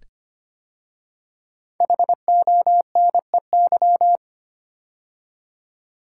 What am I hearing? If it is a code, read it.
Morse code, heard as a keyed tone, about 25 wpm, 700 Hz. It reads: HONEY
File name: 5 Letter Words - First 3 Letters Given - 25wpm